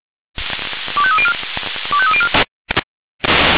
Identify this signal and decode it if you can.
Five-tone selective call (ZVEI), heard as quick runs of five tones, tones of about 70 ms. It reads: 25304 25304